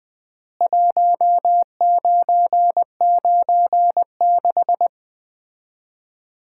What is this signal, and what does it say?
Morse code, heard as a keyed tone, about 21 wpm, 700 Hz. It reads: 1996
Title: Common QSO Phrases - 3x Word-Spacing - 20wpm